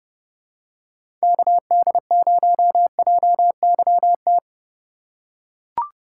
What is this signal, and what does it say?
Morse code, heard as a keyed tone, about 31 wpm, 700 Hz. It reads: KD0JYT
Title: USA Call Signs - 2x Character Spacing - 30wpm